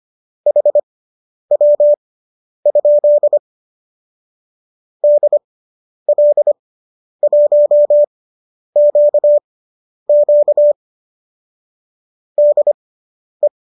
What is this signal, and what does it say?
Morse code, heard as a keyed tone, about 25 wpm, spaced as for 10 wpm, 590 Hz. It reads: HW? DL1QQ DE
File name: qso219 10